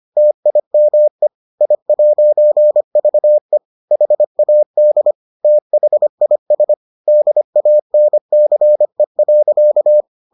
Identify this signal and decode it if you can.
Morse code, heard as a keyed tone, about 25 wpm, 600 Hz. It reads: TIME I'VE HAD THIS DANCE.